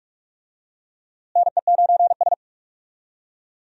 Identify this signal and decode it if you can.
Morse code, heard as a keyed tone, about 42 wpm, 700 Hz. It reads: NE9S
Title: USA Call Signs - 4 Characters - Repeated 2x - 45wpm